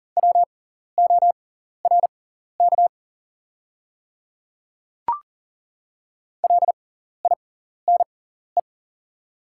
Morse code, heard as a keyed tone, about 40 wpm, spaced as for 14 wpm, 700 Hz. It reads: WORK LINE
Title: 6x Character Spacing - Top 200 Words - 40wpm